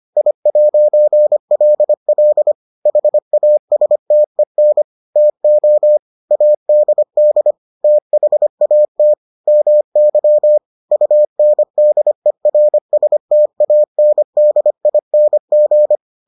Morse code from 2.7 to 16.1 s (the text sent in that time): HASTEN TO ADD THAT MY UNDERSTANDING